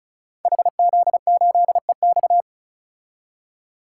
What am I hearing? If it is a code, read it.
Morse code, heard as a keyed tone, about 36 wpm, 700 Hz. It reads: HZ8EX
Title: Code Groups - 5 Letter:Numbers - 35wpm